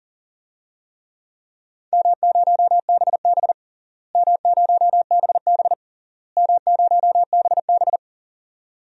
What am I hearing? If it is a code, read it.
Morse code, heard as a keyed tone, about 37 wpm, 700 Hz. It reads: M0BB M0BB M0BB
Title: DXCC-100 Call Signs - 4 Characters - Repeated 3x - 40wpm